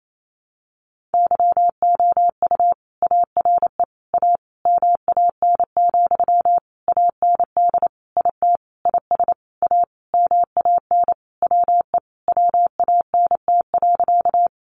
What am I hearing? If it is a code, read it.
Morse code, heard as a keyed tone, about 28 wpm, 700 Hz. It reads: YOU ARE A MAN, AND IT IS A MAN WE WANT.
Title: Sentences from Top 200 Words 28wpm